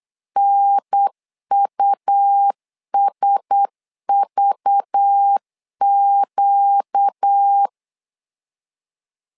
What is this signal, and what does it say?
Morse code, heard as a keyed tone, about 8 wpm, 800 Hz. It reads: NUSVQ
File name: LEZIONI / 5 LBVC / 5 LBV 8wpm 2837